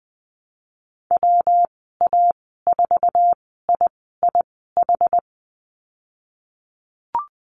Morse code, heard as a keyed tone, about 20 wpm, spaced as for 15 wpm, 700 Hz. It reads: WA4IIH
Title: 154 Lesson 25g - Letter p - Call Signs 20wpm